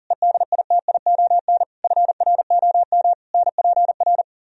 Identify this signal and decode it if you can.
Morse code, heard as a keyed tone, about 52 wpm, 700 Hz. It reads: EDITION FROM NPR